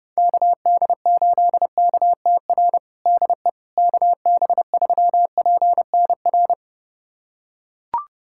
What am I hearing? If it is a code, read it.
Morse code, heard as a keyed tone, about 30 wpm, 700 Hz. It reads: KD8KTR DE KB3PNR
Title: USA Callsign DE Callsign - 6 Characters - 30wpm